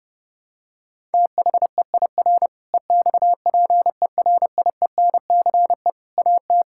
Morse code, heard as a keyed tone, about 30 wpm, 700 Hz. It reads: THEIR EXPERIENCE AT
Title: Sentences from Top 1000 Words - Review Part 6 - 30wpm